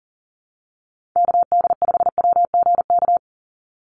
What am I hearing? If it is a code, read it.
Morse code, heard as a keyed tone, about 40 wpm, 700 Hz. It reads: KD5WGK
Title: USA Call Signs 40wpm